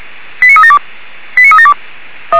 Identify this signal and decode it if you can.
Five-tone selective call (ZVEI), heard as quick runs of five tones, tones of about 70 ms. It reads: 89372 89372
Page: /464/SDR/ZVEI/89377/